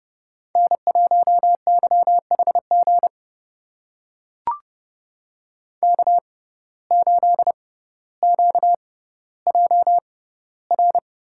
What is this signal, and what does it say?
Morse code, heard as a keyed tone, about 31 wpm, spaced as for 16 wpm, 700 Hz. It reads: N1YHG K8QJR
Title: USA Call Signs - 5 Characters - 6x Character Spacing - 30wpm